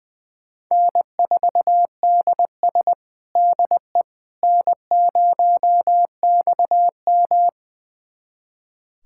Morse code, heard as a keyed tone, about 21 wpm, 700 Hz. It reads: N4DS DE N0XM
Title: USA Callsign DE Callsign - 4 Characters - 20wpm